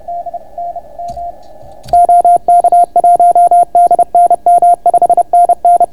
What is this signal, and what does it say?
Morse code, heard as a keyed tone, about 31 wpm, 690 Hz. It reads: D N M OK1DNM 5NN